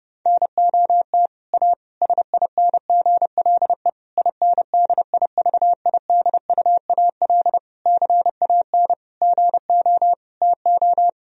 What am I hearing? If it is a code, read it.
Morse code, heard as a keyed tone, about 30 wpm, 700 Hz. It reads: NOT A SINGLE INDIVIDUAL CAN GO TO
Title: Sentences from Top 1000 Words - Review Part 4 - 30wpm